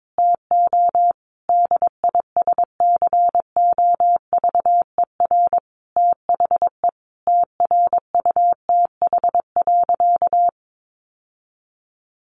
Morse code, heard as a keyed tone, about 22 wpm, 700 Hz. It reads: TO DISCOVER THE TRUTH.
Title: Sentences from Top 1100 Words 22wpm